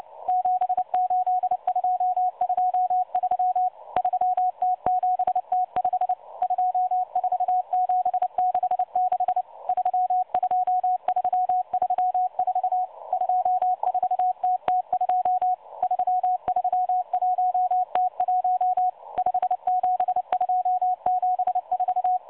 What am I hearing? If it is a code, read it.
Morse code, heard as a keyed tone, about 29 wpm, 730 Hz. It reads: 78223 3T7T5 24766 32334 24TT2 331T1 57274
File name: M12 - 8116 kHz CW 2-11-14